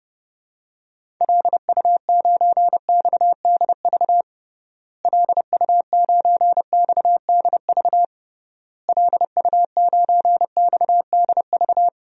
Morse code, heard as a keyed tone, about 32 wpm, 700 Hz. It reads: LU9XDV LU9XDV LU9XDV
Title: DXCC-100 Call Signs - 6 Characters - Repeated 3x - 30wpm